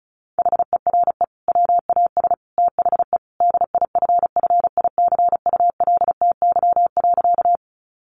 Morse code, heard as a keyed tone, about 34 wpm, 700 Hz. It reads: HERE WAS THE DIFFICULTY.
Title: Sentences from Top 1500 Words 35wpm